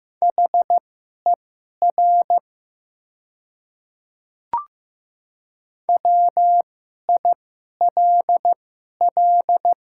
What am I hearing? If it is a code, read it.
Morse code, heard as a keyed tone, about 15 wpm, spaced as for 11 wpm, 700 Hz. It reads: HER WILL
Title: 2x Character Spacing - Top 100 Words - 15wpm